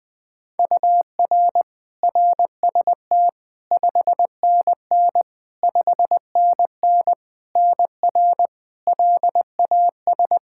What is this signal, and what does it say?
Morse code, heard as a keyed tone, about 20 wpm, 700 Hz. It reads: UR RST 5NN 5NN NR LAS
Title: K1USN SST QSO Practice 20wpm